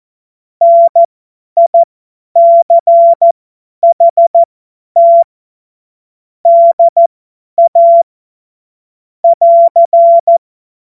Morse code, heard as a keyed tone, about 14 wpm, spaced as for 10 wpm, 680 Hz. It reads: NICHT DA <AR>
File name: Morsekurs / Witze / Witz 04. April 2023 50 F70